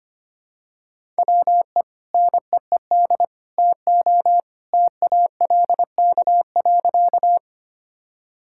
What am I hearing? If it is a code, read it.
Morse code, heard as a keyed tone, about 25 wpm, 700 Hz. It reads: WE NEED TO TALK.
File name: practice / 25wpm / Sentences with 2-Word Phrases - 25wpm